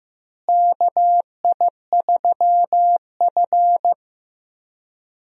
Morse code, heard as a keyed tone, about 15 wpm, 700 Hz. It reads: KI3F
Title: USA Call Signs - 4 Characters - Repeated 2x - 15wpm